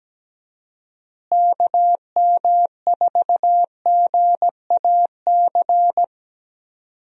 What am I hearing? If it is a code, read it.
Morse code, heard as a keyed tone, about 17 wpm, 700 Hz. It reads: KM4GAC
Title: USA Call Signs - 6 Characters - Repeated 3x - 17wpm